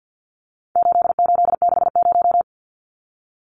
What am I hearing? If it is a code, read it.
Morse code, heard as a keyed tone, about 50 wpm, 700 Hz. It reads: 8860
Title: Numbers - 4-Digit 50wpm